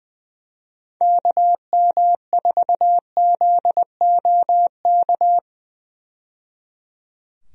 Morse code, heard as a keyed tone, about 21 wpm, 700 Hz. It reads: KM4ZOK
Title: USA Call Signs - 6 Characters - 20wpm